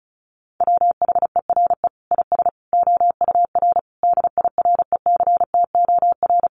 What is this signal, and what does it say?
Morse code, heard as a keyed tone, about 34 wpm, 700 Hz. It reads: WHERE IS OUR DIRECTOR
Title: Sentences from Top 1300 Words 35wpm